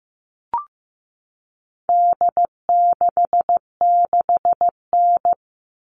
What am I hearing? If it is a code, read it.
Morse code, heard as a keyed tone, about 15 wpm, 700 Hz. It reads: D66N
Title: All Cut Numbers - 4-Digit 15wpm